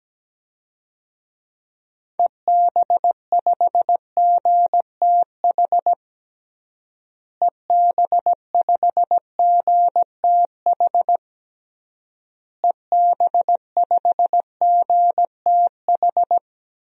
Morse code, heard as a keyed tone, about 17 wpm, 700 Hz. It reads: EB5GTH EB5GTH EB5GTH
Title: DXCC-100 Call Signs - 6 Characters - Repeated 3x - 17wpm